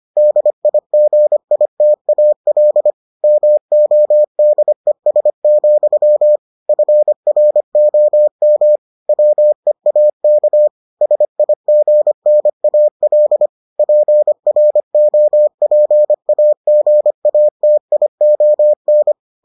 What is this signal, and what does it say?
Morse code, heard as a keyed tone, about 25 wpm, 600 Hz. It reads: DIGITAL MODES, FROM WEAK SIGNAL PROPAGATION